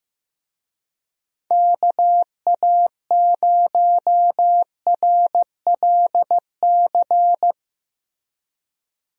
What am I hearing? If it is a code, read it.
Morse code, heard as a keyed tone, about 15 wpm, 700 Hz. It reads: KA0RLC
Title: USA Call Signs - 6 Characters - Repeated 3x - 15wpm